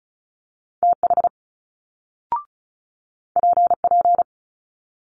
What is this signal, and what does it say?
Morse code, heard as a keyed tone, about 36 wpm, 700 Hz. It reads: TH PP
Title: Multi-letter Phonemes Quick Fire 35wpm